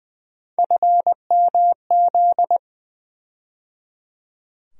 Morse code, heard as a keyed tone, about 21 wpm, 700 Hz. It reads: FMZ
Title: USA Call Sign Postfixes - 3 Letters 20wpm